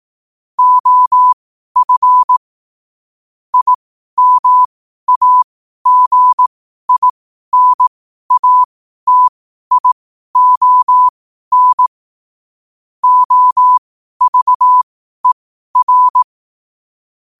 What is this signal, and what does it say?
Morse code, heard as a keyed tone, about 18 wpm, spaced as for 12 wpm, 1000 Hz. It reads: OF IMAGINATION OVER
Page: Quotes for Fri, 15 Aug 2025 in Morse Code at 12 words per minute.